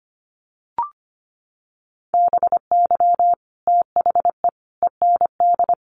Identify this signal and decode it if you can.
Morse code, heard as a keyed tone, about 25 wpm, 700 Hz. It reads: BY THE END
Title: Sets of 5 Words 25wpm